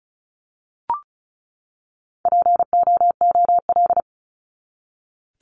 Morse code, heard as a keyed tone, about 34 wpm, 700 Hz. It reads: POOL
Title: Top 1751-2000 Words 35wpm